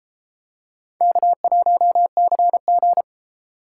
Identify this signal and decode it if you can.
Morse code, heard as a keyed tone, about 29 wpm, 700 Hz. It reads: K1CG